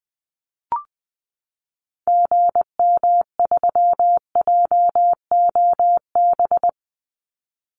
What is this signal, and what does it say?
Morse code, heard as a keyed tone, about 20 wpm, 700 Hz. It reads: GM3JOB
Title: Call Signs Repeated 2x - 20wpm